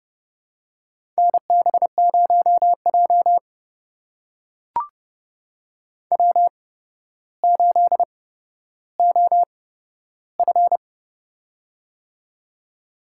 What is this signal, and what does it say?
Morse code, heard as a keyed tone, about 30 wpm, spaced as for 13 wpm, 700 Hz. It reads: NB0J W8OF
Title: USA Call Signs - 4 Characters - 8x Character Spacing - 30wpm